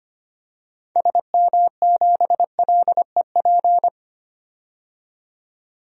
Morse code, heard as a keyed tone, about 25 wpm, 690 Hz. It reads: SM7LEP
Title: DXCC-100 Call Signs - 6 Characters - 25wpm